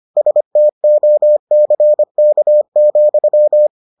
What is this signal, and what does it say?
Morse code, heard as a keyed tone, about 25 wpm, 600 Hz. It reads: STOCK,